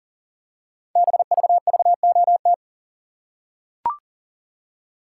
Morse code, heard as a keyed tone, about 39 wpm, 700 Hz. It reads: BVVOT
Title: Code Groups - 5 Letter:Numbers - 40wpm